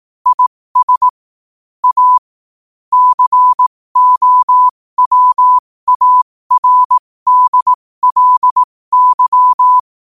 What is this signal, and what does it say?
Morse code, heard as a keyed tone, about 18 wpm, spaced as for 15 wpm, 1000 Hz. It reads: IS A COWARDLY
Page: New quotes every day in morse code at 15 Words per minute.